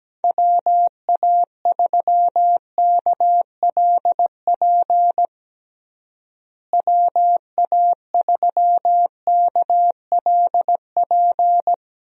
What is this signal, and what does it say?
Morse code, heard as a keyed tone, about 17 wpm, 700 Hz. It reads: WA3KLP WA3KLP
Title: USA Call Signs - 6 Characters - Repeated 2x - 17wpm